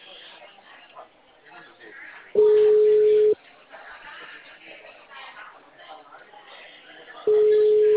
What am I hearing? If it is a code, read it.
Ringback tone (Europe, 425 Hz), with speech.
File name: out- -2030- - - .219